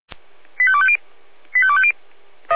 Five-tone selective call (ZVEI), heard as quick runs of five tones, tones of about 70 ms. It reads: 85250 85250